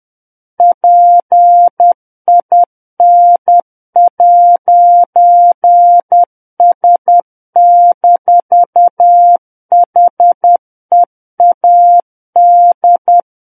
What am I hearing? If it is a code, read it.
Morse code, heard as a keyed tone, about 10 wpm, 700 Hz. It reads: PIN'S-HEAD
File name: Book One / 01-Chapter 1 - 10 WPM